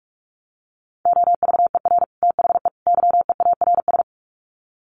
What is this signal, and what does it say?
Morse code, heard as a keyed tone, about 44 wpm, 700 Hz. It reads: OVER THE YEARS
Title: Sets of 3 Words 45wpm